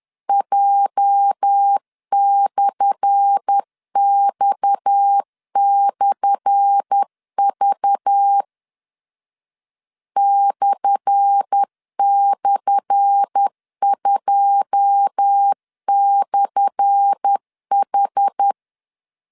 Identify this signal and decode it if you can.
Morse code, heard as a keyed tone, about 10 wpm, 800 Hz. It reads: J/X/V //2/H
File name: LEZIONI / 10 SIGNS / 10A SLASH 10wpm 2808